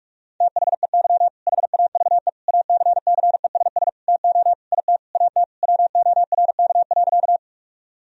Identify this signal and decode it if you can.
Morse code, heard as a keyed tone, about 46 wpm, 700 Hz. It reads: THEY HAVE ACCESS TO IT AT WORK.
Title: Sentences from Top 1000 Words - Review Part 4 - 45wpm